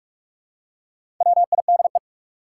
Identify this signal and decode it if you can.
Morse code, heard as a keyed tone, about 44 wpm, 700 Hz. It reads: WIDE
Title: Four Letter Words - 45wpm